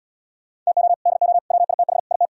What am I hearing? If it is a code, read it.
Morse code, heard as a keyed tone, about 77 wpm, 700 Hz. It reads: TO DO WITH IT